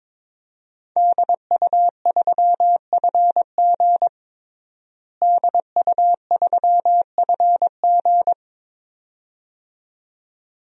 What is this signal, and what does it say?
Morse code, heard as a keyed tone, about 22 wpm, 700 Hz. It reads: DU3FG DU3FG
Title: DXCC-100 Call Signs - 5 Characters - Repeated 2x - 22wpm